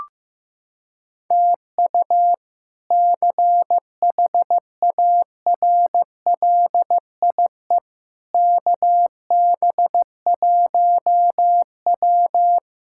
Morse code, heard as a keyed tone, about 15 wpm, 700 Hz. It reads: TU CHARLIE KB1W
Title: K1USN SST QSO Practice - 15wpm